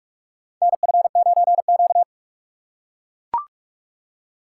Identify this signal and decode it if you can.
Morse code, heard as a keyed tone, about 43 wpm, 700 Hz. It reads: NF9Q
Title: USA Call Signs - 4 Characters - 4x Character Spacing - 45wpm